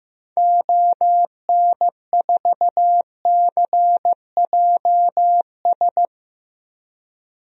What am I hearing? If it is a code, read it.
Morse code, heard as a keyed tone, about 15 wpm, 700 Hz. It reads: ON4CJS
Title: DXCC-100 Call Signs - 6 Characters - 15wpm